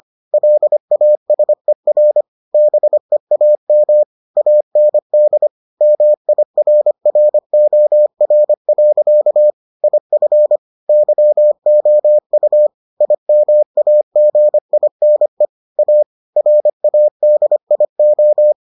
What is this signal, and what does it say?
Morse code, heard as a keyed tone, about 25 wpm, 600 Hz. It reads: LASER BEAM AND MIRROR. IF YOU IMAGINE A RADIO